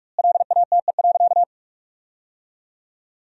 Morse code, heard as a keyed tone, about 45 wpm, 700 Hz. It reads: LATE.
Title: 2x Word-Spacing - Sentences from Top 300 Words - 45wpm